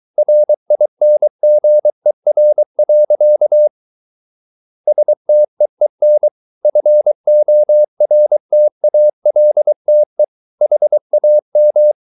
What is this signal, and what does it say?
Morse code, heard as a keyed tone, about 23 wpm, 600 Hz. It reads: RINGER. STEEN FORTALTE HAM
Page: Ferie 23 wpm